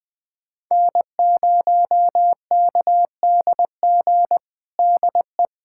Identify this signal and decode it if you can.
Morse code, heard as a keyed tone, about 20 wpm, 700 Hz. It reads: N0KDG DE
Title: USA Callsign DE Callsign 20wpm